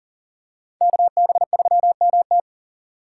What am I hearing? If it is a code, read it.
Morse code, heard as a keyed tone, about 37 wpm, 700 Hz. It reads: KB3MT
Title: USA Call Signs - 6x Character Spacing - 40wpm